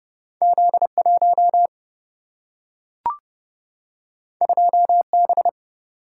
Morse code, heard as a keyed tone, about 31 wpm, 700 Hz. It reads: Z1 2B
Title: Single-Letter Single-Number 30wpm